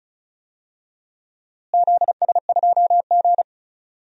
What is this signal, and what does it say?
Morse code, heard as a keyed tone, about 34 wpm, 700 Hz. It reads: ZS2G
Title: DXCC-100 Call Signs - 4 Characters - Repeated 3x - 35wpm